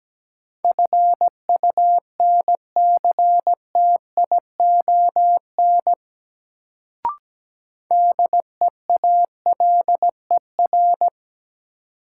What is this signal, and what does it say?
Morse code, heard as a keyed tone, about 17 wpm, 700 Hz. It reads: FUNCTION DEALER
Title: Sentences from Top 1100 Words - New Word Drill - 17wpm